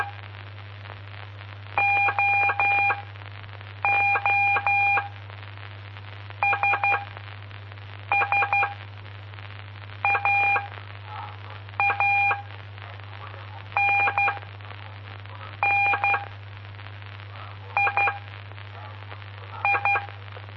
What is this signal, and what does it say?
Morse code, heard as a keyed tone, about 12 wpm, spaced as for 5 wpm, 2400 Hz. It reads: OO SS AA NN II
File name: pp5bar / aulas de telegrafia / Aula-06